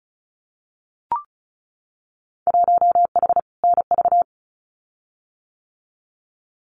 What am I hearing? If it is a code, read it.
Morse code, heard as a keyed tone, about 34 wpm, 700 Hz. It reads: 1H NV
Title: Winter Field Day Exchanges 35wpm